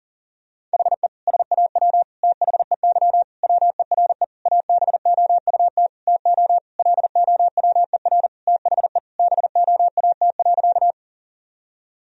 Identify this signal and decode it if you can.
Morse code, heard as a keyed tone, about 40 wpm, 700 Hz. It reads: HE SAW THEY WERE ABOUT TO LOWER THE BOAT.
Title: Sentences from Top 1000 Words - Review Part 4 - 40wpm